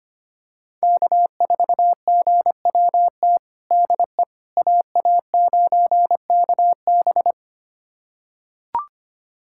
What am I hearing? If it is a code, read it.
Morse code, heard as a keyed tone, about 25 wpm, 700 Hz. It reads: K4GWT DE AA9KB
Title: USA Callsign DE Callsign 25wpm